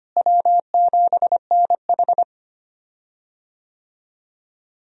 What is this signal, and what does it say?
Morse code, heard as a keyed tone, about 25 wpm, 700 Hz. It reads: W7NH
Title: USA Call Signs - 4 Characters - Repeated 3x - 25wpm